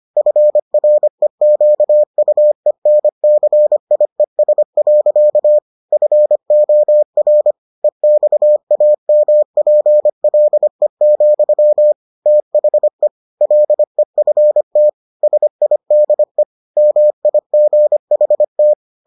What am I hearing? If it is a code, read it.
Morse code, heard as a keyed tone, about 25 wpm, 600 Hz. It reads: FREQUENCIES. FOR EXAMPLE, THE LEFT SIDE MIGHT